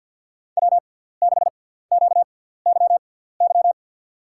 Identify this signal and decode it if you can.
Morse code, heard as a keyed tone, about 49 wpm, spaced as for 17 wpm, 700 Hz. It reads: W6QYY
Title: USA Call Signs - 5 Characters - 6x Character Spacing - 50wpm